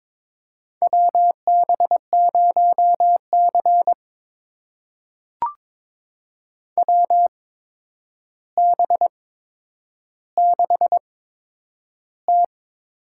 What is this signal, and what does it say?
Morse code, heard as a keyed tone, about 22 wpm, spaced as for 9 wpm, 700 Hz. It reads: WB0C WB6T
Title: USA Call Signs - 4 Characters - 8x Character Spacing - 22wpm